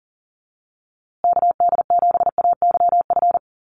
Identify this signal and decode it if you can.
Morse code, heard as a keyed tone, about 40 wpm, 700 Hz. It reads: KD7AYF
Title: USA Call Signs Repeated 3x 40wpm